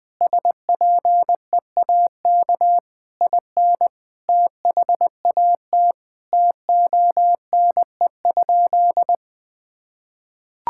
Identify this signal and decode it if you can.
Morse code, heard as a keyed tone, about 20 wpm, 700 Hz. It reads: SPEAK IN THAT TONE?
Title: Sentences from Top 1600 Words 20wpm